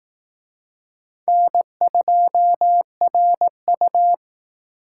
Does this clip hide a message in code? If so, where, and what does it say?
Yes: Morse code from 1.3 to 4.1 s, heard as a keyed tone, about 19 wpm, 700 Hz. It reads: N2RU